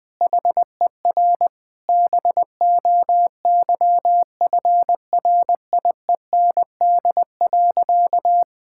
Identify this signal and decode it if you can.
Morse code, heard as a keyed tone, about 19 wpm, 700 Hz. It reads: HER BOYFRIEND.
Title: Sentences from Top 1000 Words - Review Part 5 - 20wpm